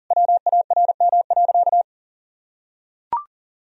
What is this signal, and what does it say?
Morse code, heard as a keyed tone, about 40 wpm, 700 Hz. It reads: WARM.
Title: Sentences from Top 1000 Words - Review Part 4 - 40wpm